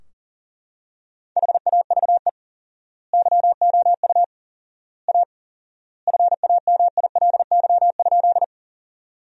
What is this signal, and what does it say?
Morse code, heard as a keyed tone, about 40 wpm, 700 Hz. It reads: HAVE YOU A FAMILY?
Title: 4x Word-Spacing - Sentences from Top 400 Words - 40wpm